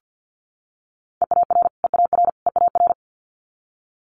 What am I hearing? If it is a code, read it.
Morse code, heard as a keyed tone, about 50 wpm, 700 Hz. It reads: EAR EAR EAR
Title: Multi-letter Phonemes Repeated 3x 50wpm